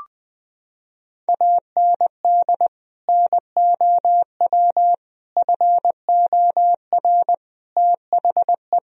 Morse code, heard as a keyed tone, about 21 wpm, 700 Hz. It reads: AND NOW FOR THE
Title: Sentences from Top 1000 Words - Review Part 1 - 20wpm